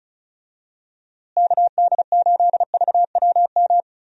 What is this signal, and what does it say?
Morse code, heard as a keyed tone, about 34 wpm, 700 Hz. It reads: KD8VWM
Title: USA Call Signs - 6 Characters - Repeated 2x - 35wpm